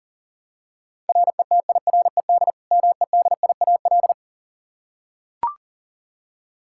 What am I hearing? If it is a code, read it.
Morse code, heard as a keyed tone, about 40 wpm, 700 Hz. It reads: RETIRED MEDIAL
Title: Practice QSO - Exchange 3 - More Info - 40wpm